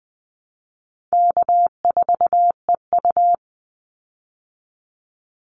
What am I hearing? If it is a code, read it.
Morse code, heard as a keyed tone, about 20 wpm, 700 Hz. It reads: K4EU
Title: Easy Call Signs Repeated 2x - 20wpm